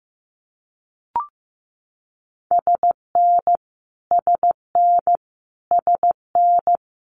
Morse code, heard as a keyed tone, about 15 wpm, 700 Hz. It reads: SN SN SN
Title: Multi-letter Phonemes Repeated 3x 15wpm